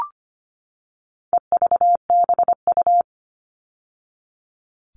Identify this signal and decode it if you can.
Morse code, heard as a keyed tone, about 25 wpm, 700 Hz. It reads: EVBU
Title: All Cut Numbers - 4-Digit 25wpm